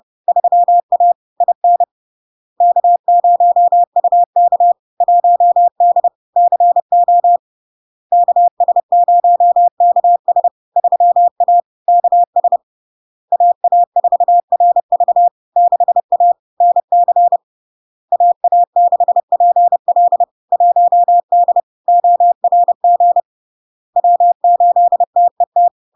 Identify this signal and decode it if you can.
Morse code, heard as a keyed tone, about 30 wpm, 700 Hz. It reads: 3A IN K0UK 1D CO KS0KS 3A KS AA4RV 6A NC AA6PL 1D ORG W8TET
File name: LARK-FD-2022-Log-in-CW-30wpm-700-Hz